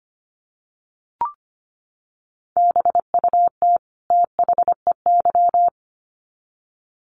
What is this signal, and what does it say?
Morse code, heard as a keyed tone, about 25 wpm, 700 Hz. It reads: BUT THEY
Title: Sets of 2 Words 25wpm